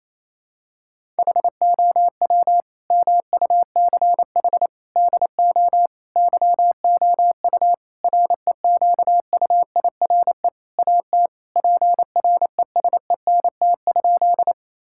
Morse code, heard as a keyed tone, about 28 wpm, 700 Hz. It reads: HOW MUCH DO YOU REQUIRE AT PRESENT?
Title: Sentences from Top 1000 Words - Review Part 6 - 28wpm